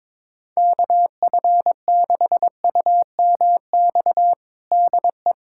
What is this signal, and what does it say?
Morse code, heard as a keyed tone, about 22 wpm, 700 Hz. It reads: KF6UMX DE
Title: USA Callsign DE Callsign - 6 Characters - 22wpm